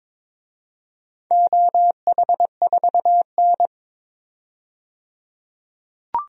Morse code, heard as a keyed tone, about 21 wpm, 700 Hz. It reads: OH4N
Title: DXCC-100 Call Signs - 4 Characters - 22wpm